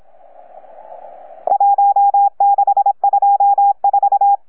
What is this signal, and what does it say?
Morse code, heard as a keyed tone, about 26 wpm, 810 Hz. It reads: 1624
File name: UP2L-20-CW